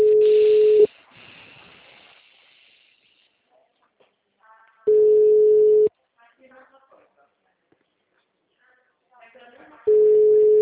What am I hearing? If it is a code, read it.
Ringback tone (Europe, 425 Hz), with speech.
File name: out- -2044- - - .444